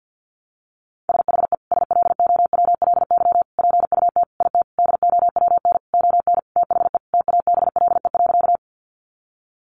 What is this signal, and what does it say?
Morse code, heard as a keyed tone, about 50 wpm, 700 Hz. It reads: SHE SLOWLY PUT IT DOWN ON THE TABLE.
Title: Sentences from Top 1100 Words 50wpm